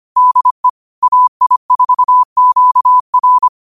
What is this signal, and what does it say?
Morse code, heard as a keyed tone, about 25 wpm, 1000 Hz. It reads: DE AI4QR
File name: qotdfiles / 25wpm